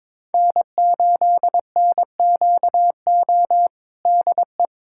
Morse code, heard as a keyed tone, about 22 wpm, 700 Hz. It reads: N8NQO DE
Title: USA Callsign DE Callsign 22wpm